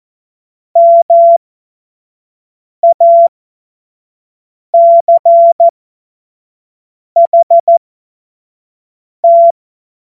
Morse code, heard as a keyed tone, about 14 wpm, spaced as for 5 wpm, 680 Hz. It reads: MACHT
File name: Morsekurs / Witze / Witz 02. November 2021, 25 F70